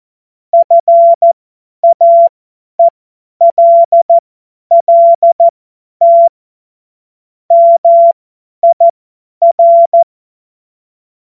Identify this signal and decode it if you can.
Morse code, heard as a keyed tone, about 14 wpm, spaced as for 10 wpm, 680 Hz. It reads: FAELLT MIR
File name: Morsekurs / Witze / Witz 18. April 2023 50 F70